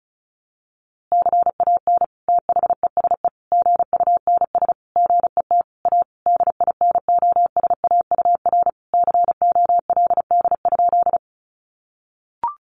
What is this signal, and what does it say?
Morse code, heard as a keyed tone, about 34 wpm, 700 Hz. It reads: CAN THESE GUNS GET A DINOSAUR COLD?
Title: A Sound Of Thunder 35wpm